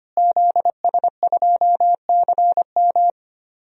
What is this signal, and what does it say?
Morse code, heard as a keyed tone, about 25 wpm, 700 Hz. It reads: ZS2CM
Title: DXCC-100 Call Signs - 5 Characters - Repeated 2x - 25wpm